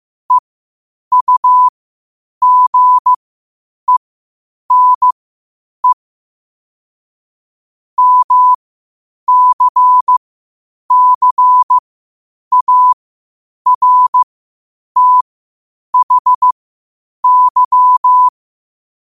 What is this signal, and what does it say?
Morse code, heard as a keyed tone, about 15 wpm, spaced as for 8 wpm, 1000 Hz. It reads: EUGENE MCCARTHY
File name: qotdfiles / 8wpm